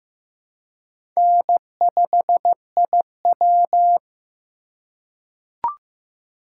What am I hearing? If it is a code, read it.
Morse code, heard as a keyed tone, about 15 wpm, 700 Hz. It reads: N5IW
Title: USA Call Signs - 4 Characters - 2x Character Spacing - 15wpm